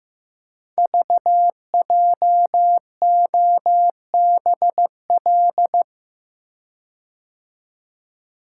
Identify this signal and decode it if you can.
Morse code, heard as a keyed tone, about 15 wpm, 700 Hz. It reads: VJOBL
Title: Code Groups - 5 Letters - 15wpm